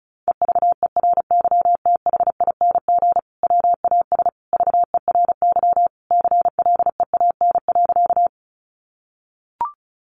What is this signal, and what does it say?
Morse code, heard as a keyed tone, about 34 wpm, 700 Hz. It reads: EVERYTHING WAS VERY CLEAN.
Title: Sentences from Top 1100 Words 35wpm